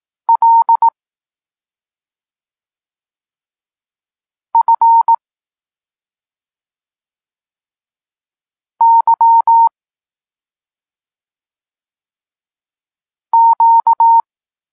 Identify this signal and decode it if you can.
Morse code, heard as a keyed tone, about 18 wpm, 920 Hz. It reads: L F Y Q
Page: Tekst 32 – Doskonalenie odbioru poznanych liter